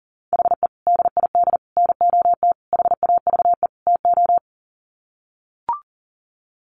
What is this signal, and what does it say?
Morse code, heard as a keyed tone, about 40 wpm, 700 Hz. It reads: HE DID NOT HAVE TO
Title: Sets of 5 Words 40wpm